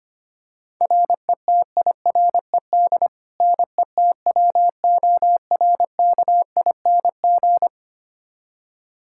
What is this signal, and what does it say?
Morse code, heard as a keyed tone, about 25 wpm, 700 Hz. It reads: RETIRED NETWORKING
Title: Practice QSO - Exchange 3 - More Info - 25wpm